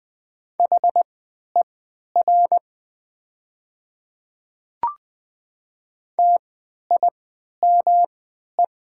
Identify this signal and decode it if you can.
Morse code, heard as a keyed tone, about 20 wpm, spaced as for 11 wpm, 700 Hz. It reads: HER TIME
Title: 3x Character Spacing - Top 100 Words - 20wpm